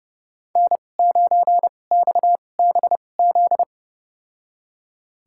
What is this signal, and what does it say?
Morse code, heard as a keyed tone, about 30 wpm, spaced as for 22 wpm, 690 Hz. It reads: N9XBZ
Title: USA Call Signs - 5 Characters - 2x Character Spacing - 30wpm